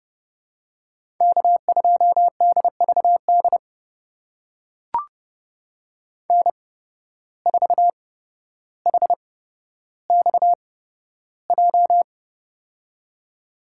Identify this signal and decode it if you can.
Morse code, heard as a keyed tone, about 31 wpm, spaced as for 13 wpm, 700 Hz. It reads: K2DVD N4HXJ
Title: USA Call Signs - 5 Characters - 8x Character Spacing - 30wpm